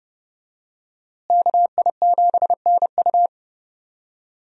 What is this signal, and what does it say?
Morse code, heard as a keyed tone, about 30 wpm, 700 Hz. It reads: KI7NU
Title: USA Call Signs - 5 Characters - Repeated 3x - 30wpm